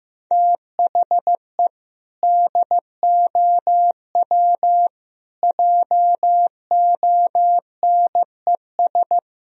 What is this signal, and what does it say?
Morse code, heard as a keyed tone, about 15 wpm, 700 Hz. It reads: THE DOW JONES
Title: Speed-Racing - Sets of 5 Words 15wpm